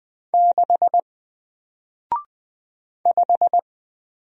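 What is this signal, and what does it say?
Morse code, heard as a keyed tone, about 20 wpm, 700 Hz. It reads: 6 5
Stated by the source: Single letter and number Morse drill at 20 wpm, from N4LCD.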